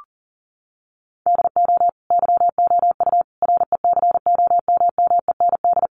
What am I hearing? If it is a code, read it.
Morse code, heard as a keyed tone, about 37 wpm, 700 Hz. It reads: DO YOU RECOMMEND
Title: Sentences from Top 1200 Words 40wpm